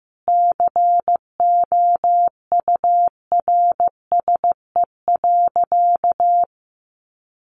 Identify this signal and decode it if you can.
Morse code, heard as a keyed tone, about 15 wpm, 700 Hz. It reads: COURSE.
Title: Sentences from Top 500 Words 15wpm